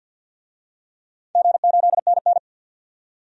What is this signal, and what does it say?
Morse code, heard as a keyed tone, about 53 wpm, 700 Hz. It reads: G8NN
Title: DXCC-100 Call Signs - 4 Characters - 50wpm